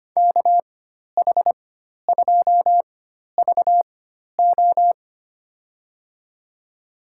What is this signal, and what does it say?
Morse code, heard as a keyed tone, about 25 wpm, spaced as for 12 wpm, 700 Hz. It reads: KH2VO
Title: USA Call Signs - 4x Character Spacing - 25wpm